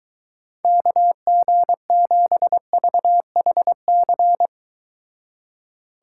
Morse code, heard as a keyed tone, about 23 wpm, 700 Hz. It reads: KG7VHC